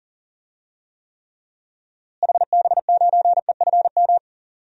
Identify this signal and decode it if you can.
Morse code, heard as a keyed tone, about 37 wpm, 700 Hz. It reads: HB9EFM
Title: DXCC-100 Call Signs - 6 Characters - Repeated 2x - 40wpm